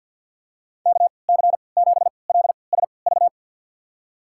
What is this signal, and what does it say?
Morse code, heard as a keyed tone, about 50 wpm, spaced as for 28 wpm, 700 Hz. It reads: KC7LSV
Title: USA Call Signs - 6 Characters - 3x Character Spacing - 50wpm